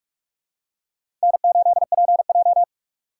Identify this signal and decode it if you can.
Morse code, heard as a keyed tone, about 46 wpm, 700 Hz. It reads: N8PJ
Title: USA Call Signs - 4 Characters - 2x Character Spacing - 45wpm